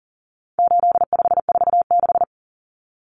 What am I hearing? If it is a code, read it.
Morse code, heard as a keyed tone, about 40 wpm, 700 Hz. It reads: 8546
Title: Common Cut Numbers - 4-Digit 40wpm